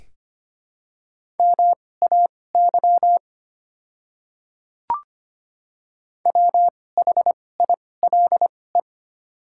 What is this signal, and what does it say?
Morse code, heard as a keyed tone, about 25 wpm, spaced as for 18 wpm, 700 Hz. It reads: MAY WHILE
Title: 2x Character Spacing - Top 500 Words - 25wpm